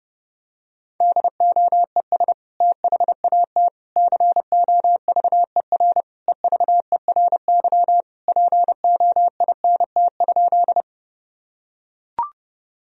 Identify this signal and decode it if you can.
Morse code, heard as a keyed tone, about 32 wpm, 700 Hz. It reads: DOES THAT COVER EVERY POINT?
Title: Sentences from Top 1000 Words - Review Part 1 - 30wpm